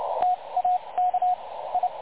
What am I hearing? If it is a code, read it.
Morse code, heard as a keyed tone, about 30 wpm, spaced as for 23 wpm, 730 Hz, with radio noise. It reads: TAK I
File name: dlazaawansowanych / nagraniazpasma / websdr recording 3565.0kHz